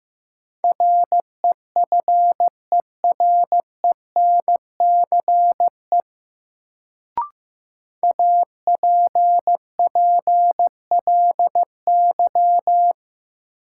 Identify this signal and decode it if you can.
Morse code, heard as a keyed tone, about 15 wpm, 700 Hz. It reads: REFERENCE APPLY
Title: Sentences from Top 1200 Words - New Word Drill - 15wpm